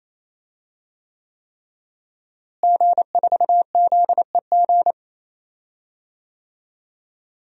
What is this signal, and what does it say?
Morse code, heard as a keyed tone, about 28 wpm, 700 Hz. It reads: G4ZEG
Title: DXCC-100 Call Signs - 5 Characters - Repeated 3x - 28wpm